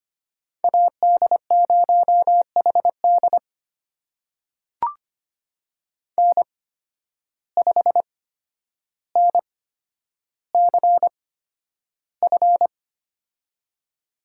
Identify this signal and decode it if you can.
Morse code, heard as a keyed tone, about 25 wpm, spaced as for 11 wpm, 700 Hz. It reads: AD0HD N5NCF
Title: USA Call Signs - 8x Character Spacing - 25wpm